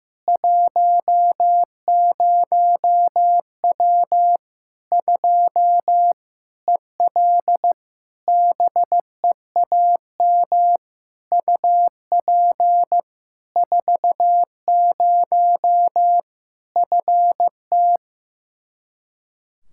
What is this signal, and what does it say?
Morse code, heard as a keyed tone, about 15 wpm, 700 Hz. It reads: 10W 2 EL BEAM UP 40 FT
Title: Practice QSO - Exchange 2 - Info - 15wpm